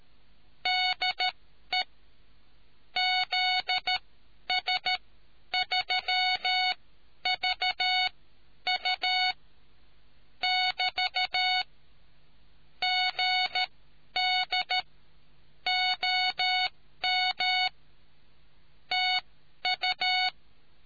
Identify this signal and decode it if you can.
Morse code, heard as a keyed tone, about 13 wpm, spaced as for 10 wpm, 740 Hz. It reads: DE ZS3VU <BT> GD OM TU